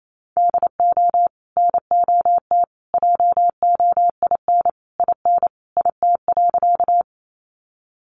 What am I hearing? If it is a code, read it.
Morse code, heard as a keyed tone, about 28 wpm, 700 Hz. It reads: DO NOT JOIN IN IT.
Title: Sentences from Top 1400 Words 28wpm